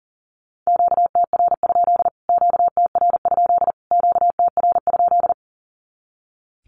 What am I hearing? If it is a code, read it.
Morse code, heard as a keyed tone, about 38 wpm, 700 Hz. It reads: QTR? QTR? QTR?
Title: Q-Codes Repeated 3x 40wpm